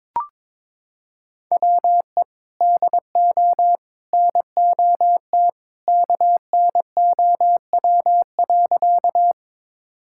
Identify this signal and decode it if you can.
Morse code, heard as a keyed tone, about 22 wpm, 700 Hz. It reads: WE DO NOT KNOW.
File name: practice / 22wpm / Sentences with 3-Word Phrases - 22wpm